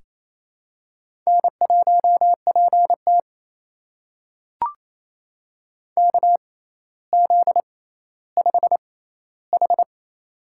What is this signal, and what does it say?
Morse code, heard as a keyed tone, about 28 wpm, spaced as for 14 wpm, 700 Hz. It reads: N1PT KZ5H
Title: USA Call Signs - 4 Characters - 6x Character Spacing - 28wpm